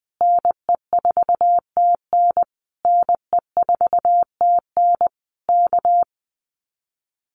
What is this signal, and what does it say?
Morse code, heard as a keyed tone, about 20 wpm, 700 Hz. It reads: NE4TN NE4TN K
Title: Common QSO Phrases Encore 20wpm